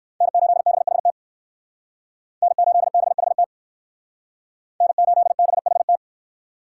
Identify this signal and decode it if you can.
Morse code, heard as a keyed tone, about 58 wpm, 700 Hz. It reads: N8BHT N8BHT N8BHT